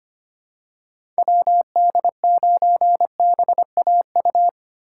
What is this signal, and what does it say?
Morse code, heard as a keyed tone, about 25 wpm, 700 Hz. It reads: WD9BAU
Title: USA Call Signs - 6 Characters - 3x Character Spacing - 25wpm